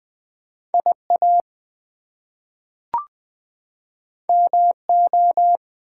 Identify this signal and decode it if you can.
Morse code, heard as a keyed tone, about 20 wpm, 700 Hz. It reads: IA MO
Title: US States and Canadian Provinces 20wpm